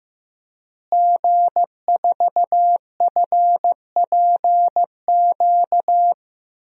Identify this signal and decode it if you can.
Morse code, heard as a keyed tone, about 15 wpm, 700 Hz. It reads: G4FPQ
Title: DXCC-100 Call Signs - 5 Characters - Repeated 3x - 15wpm